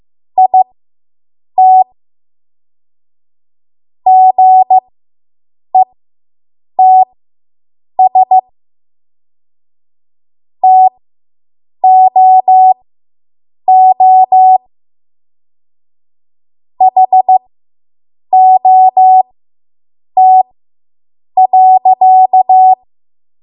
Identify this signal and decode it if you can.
Morse code, heard as a keyed tone, about 15 wpm, spaced as for 7 wpm, 750 Hz. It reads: IT GETS TOO HOT.